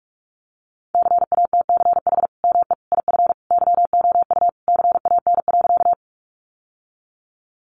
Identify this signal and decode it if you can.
Morse code, heard as a keyed tone, about 44 wpm, 700 Hz. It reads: CATCH ME IF YOU CAN.
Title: Sentences from Top 1500 Words 45wpm